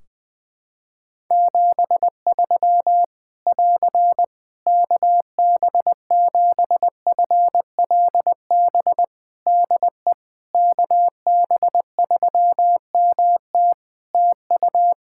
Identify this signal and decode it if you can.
Morse code, heard as a keyed tone, about 21 wpm, 700 Hz. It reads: 73 <AR> KB7FLB DE KB3MT TU
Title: Practice QSO - Exchange 4 - Ending - 20wpm